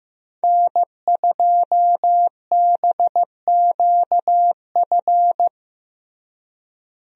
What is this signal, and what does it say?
Morse code, heard as a keyed tone, about 15 wpm, 700 Hz. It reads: N2BQF
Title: USA Call Signs - 5 Characters - Repeated 3x - 15wpm